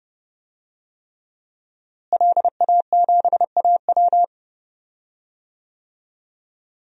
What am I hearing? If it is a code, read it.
Morse code, heard as a keyed tone, about 30 wpm, 700 Hz. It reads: LA7AW
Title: DXCC-100 Call Signs - 5 Characters - Repeated 3x - 30wpm